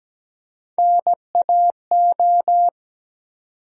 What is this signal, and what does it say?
Morse code, heard as a keyed tone, about 17 wpm, 700 Hz. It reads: NAO
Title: USA Call Sign Postfixes - 3 Letters 17wpm